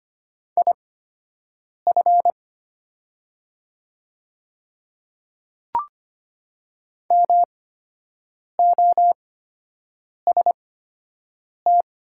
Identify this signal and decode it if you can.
Morse code, heard as a keyed tone, about 25 wpm, spaced as for 7 wpm, 700 Hz. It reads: IF MOST
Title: 8x Character Spacing - Top 100 Words - 25wpm